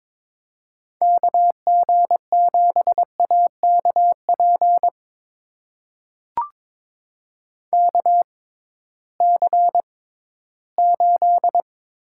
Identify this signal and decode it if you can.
Morse code, heard as a keyed tone, about 21 wpm, spaced as for 14 wpm, 700 Hz. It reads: KG7AKP KC8
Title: USA Call Signs - 6 Characters - 6x Character Spacing - 22wpm